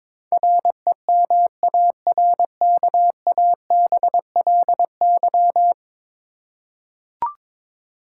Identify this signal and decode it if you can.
Morse code, heard as a keyed tone, about 21 wpm, 700 Hz. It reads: REMARKABLY
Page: morse-code-ninja-practice-22wpm